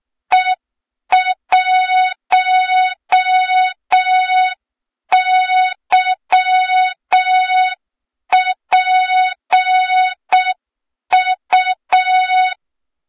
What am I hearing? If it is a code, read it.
Morse code, heard as a keyed tone, about 6 wpm, 3700 Hz. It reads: E1YPU
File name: Morze-HA2MN / hang / 23 GYAKORLAT